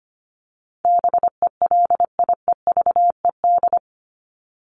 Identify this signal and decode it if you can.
Morse code, heard as a keyed tone, about 25 wpm, 700 Hz. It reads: BELIEVED
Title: ED Suffix - Top 100 Words with Base Words 25wpm